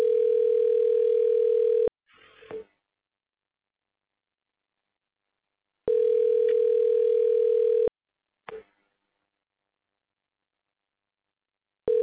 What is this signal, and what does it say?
Ringback tone (North America), 440 + 480 Hz.
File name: recordings / RE0aeb2d5be6117d25f99bbcf7fd2120b8